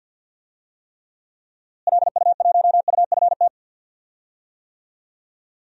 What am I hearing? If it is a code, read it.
Morse code, heard as a keyed tone, about 46 wpm, 700 Hz. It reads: LU1UFT
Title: DXCC-100 Call Signs - 6 Characters - Repeated 2x - 50wpm